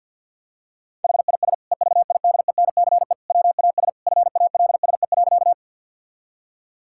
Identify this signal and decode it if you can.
Morse code, heard as a keyed tone, about 50 wpm, 700 Hz. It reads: HIS EVIDENCE WAS FALSE.
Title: Sentences from Top 1600 Words 50wpm